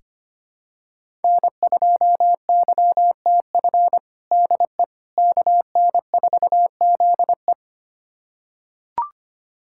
Morse code, heard as a keyed tone, about 25 wpm, 700 Hz. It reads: N2YTF DE KN4ZE
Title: USA Callsign DE Callsign - 5 Characters - 25wpm